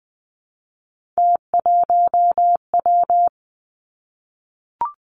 Morse code, heard as a keyed tone, about 20 wpm, 700 Hz. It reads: T1W
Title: Three Challenging Letter-Number Combinations 20wpm